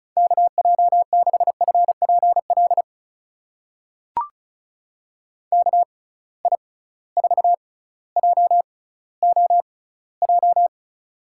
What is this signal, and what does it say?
Morse code, heard as a keyed tone, about 36 wpm, spaced as for 18 wpm, 700 Hz. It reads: KJ6FPL KI4JOJ
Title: USA Call Signs - 6 Characters - 6x Character Spacing - 35wpm